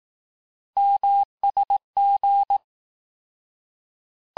Morse code, heard as a keyed tone, about 18 wpm, 800 Hz. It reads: MSG
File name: cours-du-lundi files / cours-1852-abreviations-vitesse-17MPM-puis-22MPM